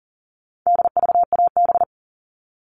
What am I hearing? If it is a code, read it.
Morse code, heard as a keyed tone, about 40 wpm, 700 Hz. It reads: DVAB
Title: All Cut Numbers - 4-Digit 40wpm